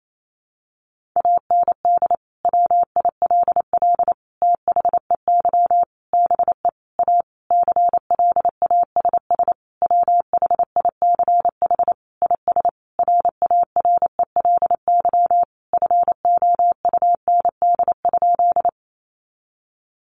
Morse code, heard as a keyed tone, about 28 wpm, 700 Hz. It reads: AND WILL THEY BE A CLASS WHICH IS RARELY FOUND?
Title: Sentences from Top 1400 Words 28wpm